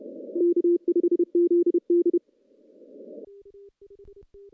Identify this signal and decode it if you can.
Morse code, heard as a keyed tone, about 31 wpm, 350 Hz. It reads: K5ZD
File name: contests / CW-SS-k5tr- -0700